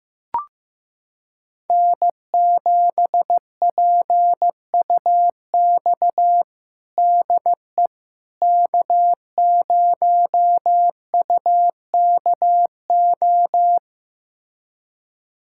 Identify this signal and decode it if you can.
Morse code, heard as a keyed tone, about 15 wpm, 700 Hz. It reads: N7PUX DE K0UKO
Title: USA Callsign DE Callsign - 5 Characters - 15wpm